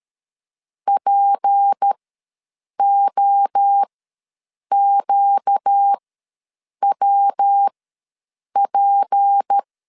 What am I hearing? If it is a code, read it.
Morse code, heard as a keyed tone, about 13 wpm, spaced as for 7 wpm, 800 Hz. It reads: POQWP
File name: LEZIONI / 6 PKX0 / 6 P ws 12wpm 3117